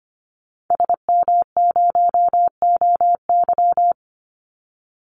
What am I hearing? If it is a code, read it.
Morse code, heard as a keyed tone, about 25 wpm, 700 Hz. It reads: SM0OY
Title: Call Signs - 25wpm